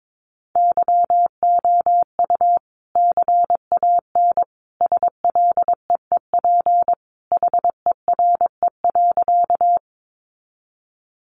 Morse code, heard as a keyed tone, about 22 wpm, 700 Hz. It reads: YOU CAN SLEEP HERE.
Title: Sentences from Top 500 Words 22wpm